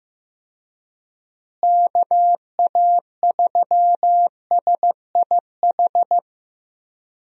Morse code, heard as a keyed tone, about 15 wpm, 700 Hz. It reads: KA3SIH
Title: USA Call Signs - 6 Characters - Repeated 3x - 15wpm